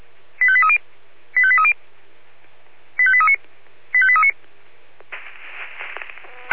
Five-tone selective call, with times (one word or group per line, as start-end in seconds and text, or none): 0.4-0.8 85830
1.4-1.7 85830
3.0-3.4 85839
3.9-4.3 85839